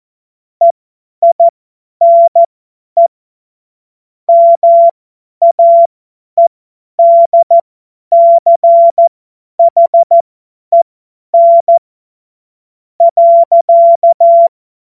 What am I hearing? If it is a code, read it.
Morse code, heard as a keyed tone, about 14 wpm, spaced as for 10 wpm, 680 Hz. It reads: EINE MAEDCHEN .
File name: Morsekurs / Witze / Witz 30. November 2021, 50 F70